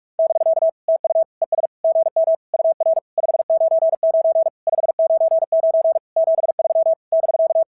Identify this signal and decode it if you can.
Morse code, heard as a keyed tone, about 44 wpm, 690 Hz. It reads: <BK> TU ES GM UR 599 599 73 <BK>
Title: SOTA QSO Practice 45wpm